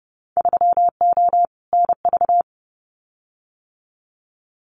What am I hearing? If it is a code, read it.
Morse code, heard as a keyed tone, about 30 wpm, 700 Hz. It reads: 3O NV
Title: Winter Field Day Exchanges Encore 30wpm